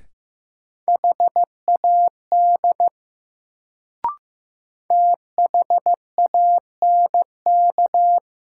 Morse code, heard as a keyed tone, about 15 wpm, 700 Hz. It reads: HAD THANK
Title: Sets of 2 Words - New Word Drill - 15wpm